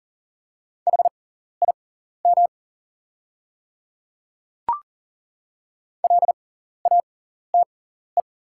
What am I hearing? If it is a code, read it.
Morse code, heard as a keyed tone, about 40 wpm, spaced as for 14 wpm, 690 Hz. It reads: HIM LATE
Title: 6x Character Spacing - Top 300 Words - 40wpm